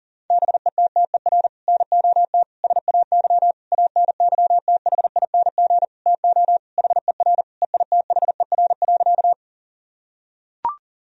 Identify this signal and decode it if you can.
Morse code, heard as a keyed tone, about 40 wpm, 700 Hz. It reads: BETTER NOT SAY ANYTHING TO HER EITHER.
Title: Sentences from Top 1000 Words - Review Part 4 - 40wpm